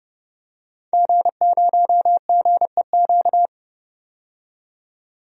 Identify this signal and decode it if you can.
Morse code, heard as a keyed tone, about 29 wpm, 700 Hz. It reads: G0GEQ
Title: DXCC-100 Call Signs - 5 Characters - 30wpm